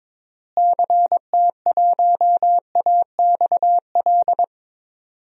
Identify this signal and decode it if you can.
Morse code, heard as a keyed tone, about 22 wpm, 700 Hz. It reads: CT1AXL
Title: DXCC-100 Call Signs - 6 Characters - Repeated 3x - 22wpm